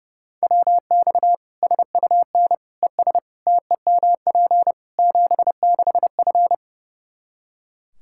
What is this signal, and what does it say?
Morse code, heard as a keyed tone, about 30 wpm, 700 Hz. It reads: WX SUN ES TEMP 76F
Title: Practice QSO - Exchange 2 - Info - 30wpm